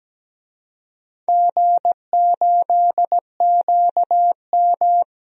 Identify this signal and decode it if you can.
Morse code, heard as a keyed tone, about 17 wpm, 700 Hz. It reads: G8QM
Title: DXCC-100 Call Signs - 4 Characters - 17wpm